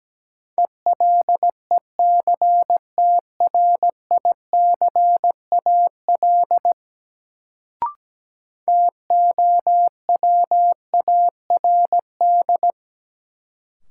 Morse code, heard as a keyed tone, about 17 wpm, 700 Hz. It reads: ELECTRICAL TOWARD
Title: Sentences from Top 1000 Words - Review Part 6 - New Word Drill - 17wpm